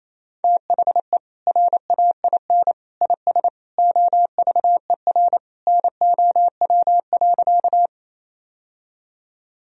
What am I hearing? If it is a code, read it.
Morse code, heard as a keyed tone, about 28 wpm, 700 Hz. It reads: THE RAIN IS OVER NOW.
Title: Sentences from Top 1000 Words - Review Part 1 - 28wpm